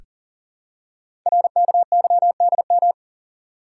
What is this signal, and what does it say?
Morse code, heard as a keyed tone, about 41 wpm, 700 Hz. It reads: RKYDM
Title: Code Groups - 5 Letters - 40wpm